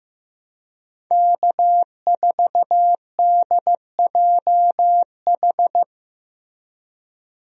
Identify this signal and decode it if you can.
Morse code, heard as a keyed tone, about 15 wpm, 700 Hz. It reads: K4DJH
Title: USA Call Signs - 5 Characters - Repeated 2x - 15wpm